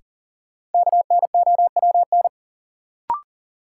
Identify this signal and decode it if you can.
Morse code, heard as a keyed tone, about 40 wpm, 700 Hz. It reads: KNOWN
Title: Sentences from Top 600 Words - New Word Drill - 40wpm